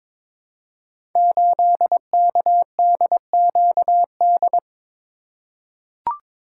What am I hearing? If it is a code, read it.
Morse code, heard as a keyed tone, about 22 wpm, 700 Hz. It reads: 8KDQD
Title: Code Groups - 5 Letter:Numbers - 22wpm